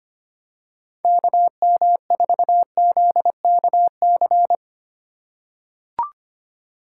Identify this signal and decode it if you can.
Morse code, heard as a keyed tone, about 25 wpm, 700 Hz. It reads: KM4ZKC
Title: USA Call Signs - 2x Character Spacing - 25wpm